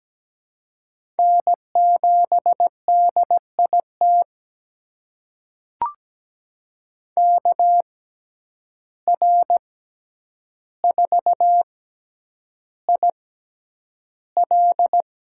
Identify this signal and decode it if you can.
Morse code, heard as a keyed tone, about 17 wpm, spaced as for 9 wpm, 700 Hz. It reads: N7DIT KR4IL
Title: USA Call Signs - 5 Characters - 6x Character Spacing - 17wpm